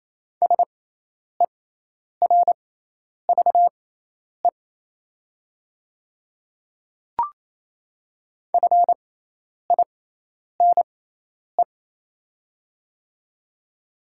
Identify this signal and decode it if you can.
Morse code, heard as a keyed tone, about 28 wpm, spaced as for 10 wpm, 700 Hz. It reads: SERVE FINE
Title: 6x Character Spacing - Top 500 Words - 28wpm